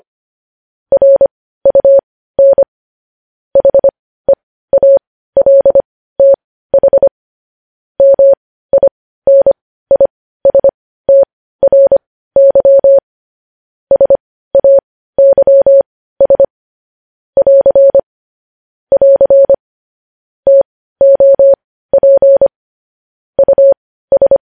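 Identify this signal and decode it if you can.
Morse code, heard as a keyed tone, about 25 wpm, spaced as for 15 wpm, 550 Hz. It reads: RUN HEALTH MINISTRY SAYS <AR> <AR> TOP US
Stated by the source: Morse code news headlines at 15.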